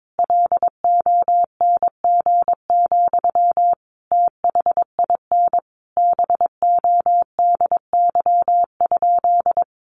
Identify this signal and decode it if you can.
Morse code, heard as a keyed tone, about 22 wpm, 700 Hz. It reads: LONG, THIN BODY?
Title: Sentences from Top 1200 Words 22wpm